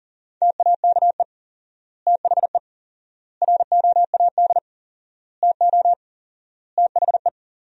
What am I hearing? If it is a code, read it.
Morse code, heard as a keyed tone, about 37 wpm, 700 Hz. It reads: TAKE THE ROAD TO THE
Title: 4x Word-Spacing - Sentences from Top 400 Words - 40wpm